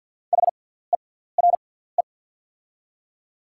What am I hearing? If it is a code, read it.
Morse code, heard as a keyed tone, about 48 wpm, spaced as for 17 wpm, 690 Hz. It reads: HERE
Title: 6x Character Spacing - Top 200 Words - 50wpm